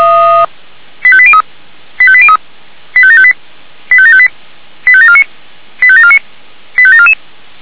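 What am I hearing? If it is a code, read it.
Five-tone selective call (ZVEI), heard as quick runs of five tones, tones of about 70 ms. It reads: 85803 85803 85758 85758 85749 85749 85744
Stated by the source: ZVEI five-tone selective call received on an SDR.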